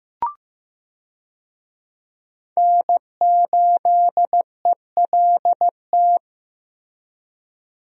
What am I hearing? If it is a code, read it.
Morse code, heard as a keyed tone, about 14 wpm, 700 Hz. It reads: N8ELT
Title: USA Call Signs - 5 Characters - Repeated 3x - 15wpm